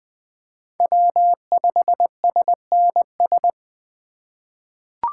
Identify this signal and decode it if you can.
Morse code, heard as a keyed tone, about 20 wpm, 700 Hz. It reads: W5SNS
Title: USA Call Signs - 5 Characters - 2x Character Spacing - 20wpm